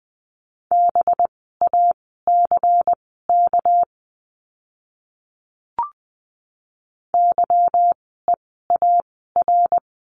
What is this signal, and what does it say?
Morse code, heard as a keyed tone, about 20 wpm, spaced as for 15 wpm, 700 Hz. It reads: BACK YEAR
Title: 244 Lesson 39f - Letter z - Words 20wpm